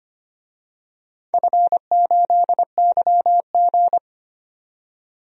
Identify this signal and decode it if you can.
Morse code, heard as a keyed tone, about 25 wpm, 700 Hz. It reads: F8YG
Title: DXCC-100 Call Signs - 4 Characters - 25wpm